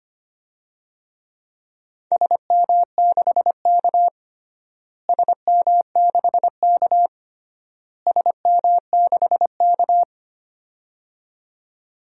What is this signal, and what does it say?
Morse code, heard as a keyed tone, about 25 wpm, 700 Hz. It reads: SM6K SM6K SM6K
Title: DXCC-100 Call Signs - 4 Characters - Repeated 3x - 25wpm